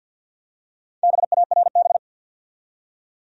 Morse code, heard as a keyed tone, about 50 wpm, 700 Hz. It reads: BARB
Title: USA Common Names - Max 5 Letters - 50wpm